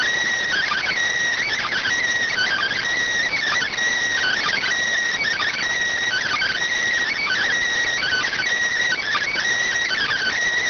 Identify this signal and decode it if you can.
Emergency Alert System (SAME) tones (header bursts), heard as harsh buzzing bursts. It is heard from 0.5 to 10.4 s.
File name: findings / MR 2 Pel Sag T2 SSFSE 0009.gen-38 rx